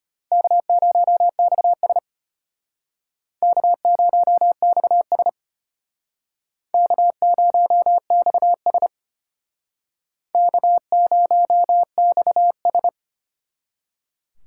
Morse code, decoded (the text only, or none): K0XS K0XS K0XS K0XS